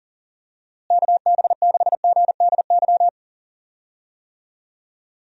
Morse code, heard as a keyed tone, about 40 wpm, 700 Hz. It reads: KB6GDY
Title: USA Call Signs - 6 Characters - Repeated 3x - 40wpm